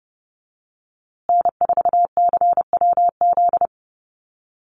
Morse code, heard as a keyed tone, about 30 wpm, 700 Hz. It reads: N4CWZ
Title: USA Call Signs Repeated 3x 30wpm